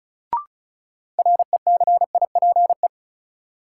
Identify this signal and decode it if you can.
Morse code, heard as a keyed tone, about 36 wpm, 700 Hz. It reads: RECIPE
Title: Sentences from Top 1200 Words - New Word Drill Alt - 35wpm